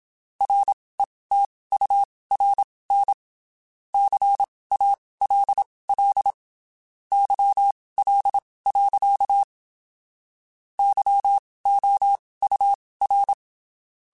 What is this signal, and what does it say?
Morse code, heard as a keyed tone, about 27 wpm, spaced as for 19 wpm, 800 Hz. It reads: RETURN CALL YL. YOUR
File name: QSOs Spaced